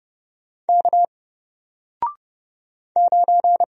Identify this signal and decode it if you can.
Morse code, heard as a keyed tone, about 30 wpm, 700 Hz. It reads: K 9
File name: Single-Letter-Number-30wpm